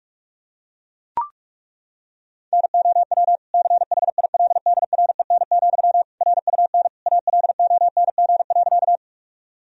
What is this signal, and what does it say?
Morse code, heard as a keyed tone, about 41 wpm, 700 Hz. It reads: NOW CHILDREN, RUN ALONG.
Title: Sentences from Top 1000 Words - Review Part 1 - 45wpm